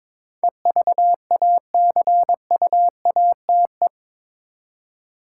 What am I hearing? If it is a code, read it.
Morse code, heard as a keyed tone, about 22 wpm, 700 Hz. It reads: EVACUATE
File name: practice / 22wpm / Eight Letter Words - 22wpm